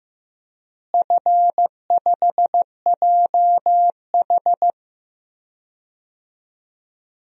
Morse code, heard as a keyed tone, about 15 wpm, 700 Hz. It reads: F5JH
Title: DXCC-100 Call Signs - 4 Characters - Repeated 2x - 15wpm